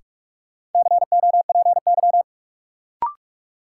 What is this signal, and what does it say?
Morse code, heard as a keyed tone, about 45 wpm, 700 Hz. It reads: COPY
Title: Sentences from Top 1400 Words - New Word Drill Alt - 45wpm